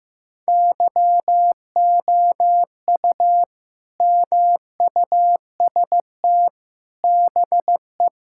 Morse code, heard as a keyed tone, about 16 wpm, 700 Hz. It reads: YOU MUST BE
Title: Sentences from Top 1600 Words 15wpm